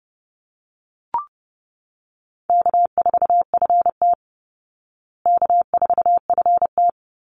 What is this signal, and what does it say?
Morse code, heard as a keyed tone, about 28 wpm, 700 Hz. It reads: K4FT K4FT
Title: Easy Call Signs Repeated 2x - 30wpm